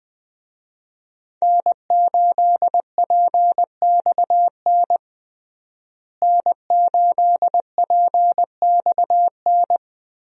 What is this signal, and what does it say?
Morse code, heard as a keyed tone, about 20 wpm, 700 Hz. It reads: N8PXN N8PXN
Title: USA Call Signs - 5 Characters - Repeated 3x - 20wpm